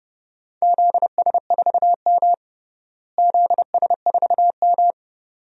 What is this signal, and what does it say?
Morse code, heard as a keyed tone, about 31 wpm, 700 Hz. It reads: ZS4M ZS4M
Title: DXCC-100 Call Signs - 4 Characters - Repeated 2x - 30wpm